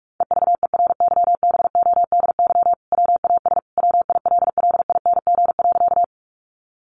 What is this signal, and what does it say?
Morse code, heard as a keyed tone, about 44 wpm, 700 Hz. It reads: EVERYBODY WAS WILLING.
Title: Sentences from Top 1400 Words 45wpm